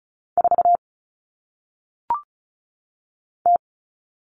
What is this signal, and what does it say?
Morse code, heard as a keyed tone, about 36 wpm, 700 Hz. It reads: 4 T
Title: Common Cut Numbers - 1-Digit 35wpm